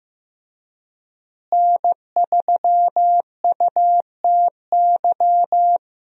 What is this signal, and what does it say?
Morse code, heard as a keyed tone, about 15 wpm, 700 Hz. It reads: N3UTY
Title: USA Call Signs - 5 Characters - Repeated 2x - 15wpm